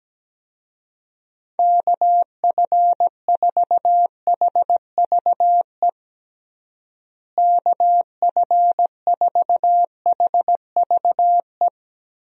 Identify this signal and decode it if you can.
Morse code, heard as a keyed tone, about 17 wpm, 700 Hz. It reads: KF4HVE KF4HVE
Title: USA Call Signs - 6 Characters - Repeated 2x - 17wpm